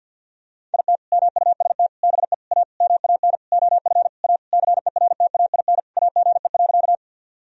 Morse code, heard as a keyed tone, about 50 wpm, 700 Hz. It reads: IT MUST BE A MAN OF A CERTAIN AGE.
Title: Sentences from Top 1000 Words - Review Part 3 - 50wpm